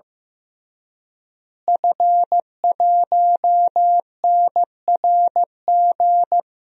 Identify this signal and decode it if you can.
Morse code, heard as a keyed tone, about 14 wpm, 700 Hz. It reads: F1NRG
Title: DXCC-100 Call Signs - 5 Characters - Repeated 3x - 15wpm